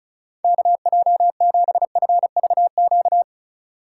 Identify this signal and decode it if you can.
Morse code, heard as a keyed tone, about 34 wpm, 700 Hz. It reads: KJ7FVQ
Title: USA Call Signs - 6 Characters - 4x Character Spacing - 35wpm